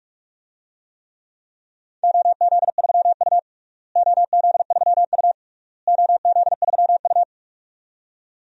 Morse code, heard as a keyed tone, about 46 wpm, 700 Hz. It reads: OZ3U OZ3U OZ3U
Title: DXCC-100 Call Signs - 4 Characters - Repeated 3x - 45wpm